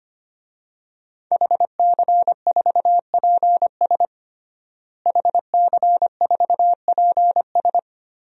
Morse code, heard as a keyed tone, about 25 wpm, 700 Hz. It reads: HC4PS HC4PS
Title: DXCC-100 Call Signs - 5 Characters - Repeated 3x - 25wpm